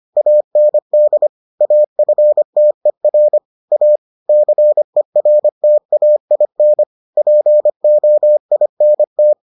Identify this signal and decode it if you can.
Morse code, heard as a keyed tone, about 25 wpm, 600 Hz. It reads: AND AFTER A CERTAIN POINT